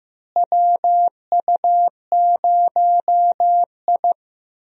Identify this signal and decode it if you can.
Morse code, heard as a keyed tone, about 15 wpm, 700 Hz. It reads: WU0I
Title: USA Call Signs - 4 Characters - Repeated 3x - 15wpm